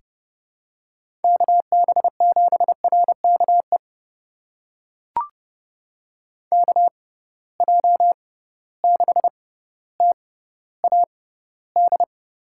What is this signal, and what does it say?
Morse code, heard as a keyed tone, about 30 wpm, spaced as for 15 wpm, 700 Hz. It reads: KB7RKE KJ6TAD
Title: USA Call Signs - 6 Characters - 6x Character Spacing - 30wpm